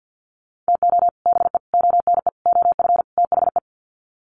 Morse code, heard as a keyed tone, about 50 wpm, 700 Hz. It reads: TO BE ONE OF THE
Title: Sets of 5 Words 50wpm